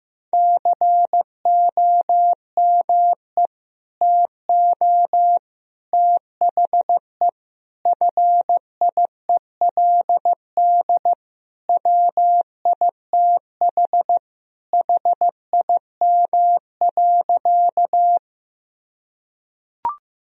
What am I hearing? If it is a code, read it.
Morse code, heard as a keyed tone, about 15 wpm, 700 Hz. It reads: COME TO THE FIELD WITH HIM.
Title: Sentences from Top 1000 Words - Review Part 2 - 15wpm